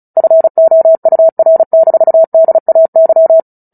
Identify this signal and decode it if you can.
Morse code, heard as a keyed tone, about 34 wpm, 650 Hz. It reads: FOUR-DAY
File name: Book Two / 09-Chapter 9 - 35 WPM